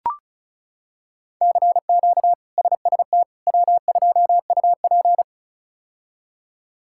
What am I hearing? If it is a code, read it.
Morse code, heard as a keyed tone, about 34 wpm, 700 Hz. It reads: CQ SST W2UP
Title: K1USN SST QSO Practice - 35wpm